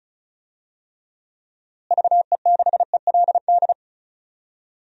Morse code, heard as a keyed tone, about 36 wpm, 700 Hz. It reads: VE6ELD
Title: DXCC-100 Call Signs - 6 Characters - Repeated 3x - 35wpm